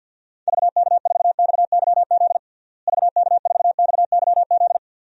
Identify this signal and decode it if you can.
Morse code, heard as a keyed tone, about 53 wpm, 700 Hz. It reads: VK4XYZ VK4XYZ
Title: DXCC-100 Call Signs - 6 Characters - Repeated 3x - 50wpm